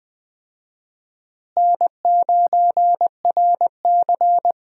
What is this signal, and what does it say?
Morse code, heard as a keyed tone, about 20 wpm, 700 Hz. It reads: N9RC
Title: USA Call Signs - 4 Characters - Repeated 3x - 20wpm